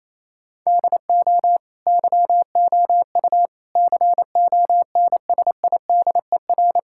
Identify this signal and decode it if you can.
Morse code, heard as a keyed tone, about 28 wpm, 700 Hz. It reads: DO YOU CONSIDER
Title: Sentences from Top 1000 Words - Review Part 6 - 28wpm